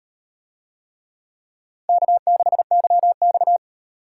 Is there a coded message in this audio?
Yes: Morse code from 1.9 to 3.6 s, heard as a keyed tone, about 34 wpm, 700 Hz. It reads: K6YX